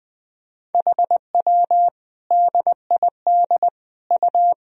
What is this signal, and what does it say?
Morse code, heard as a keyed tone, about 20 wpm, 700 Hz. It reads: HW DID U
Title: Sentences from Top 100 Words and Abbreviations 20wpm